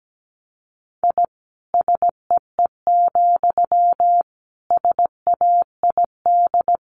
Morse code, heard as a keyed tone, about 17 wpm, 700 Hz. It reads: I SEE, SAID
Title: A Sound Of Thunder 17wpm